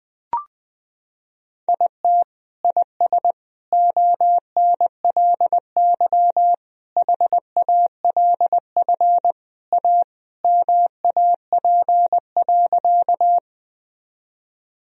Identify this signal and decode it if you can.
Morse code, heard as a keyed tone, about 20 wpm, 700 Hz. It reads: IT IS ONLY HALF A MAP.
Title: Sentences from Top 1000 Words - Review Part 2 - 20wpm